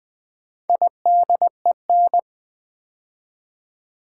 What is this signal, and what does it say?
Morse code, heard as a keyed tone, about 20 wpm, 700 Hz. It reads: IDEN
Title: Build A Word - 2 Letters At A Time - 8-Letter Words - 20wpm